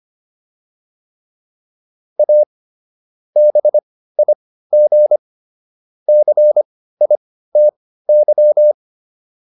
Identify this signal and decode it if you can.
Morse code, heard as a keyed tone, about 25 wpm, spaced as for 15 wpm, 600 Hz. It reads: A BIG CITY
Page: intermediate-practice-files